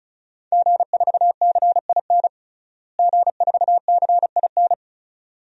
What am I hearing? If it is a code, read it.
Morse code, heard as a keyed tone, about 36 wpm, 700 Hz. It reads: G4CIN G4CIN
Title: DXCC-100 Call Signs - 5 Characters - Repeated 2x - 35wpm